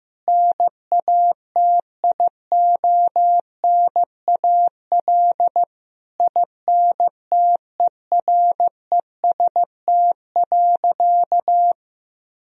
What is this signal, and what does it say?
Morse code, heard as a keyed tone, about 15 wpm, 700 Hz. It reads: NATIONAL INTEREST.
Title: Sentences from Top 1000 Words - Review Part 4 - 15wpm